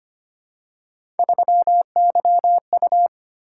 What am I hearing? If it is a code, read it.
Morse code, heard as a keyed tone, about 25 wpm, 700 Hz. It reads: 3YU
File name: Three Letter-Number Combinations - 25wpm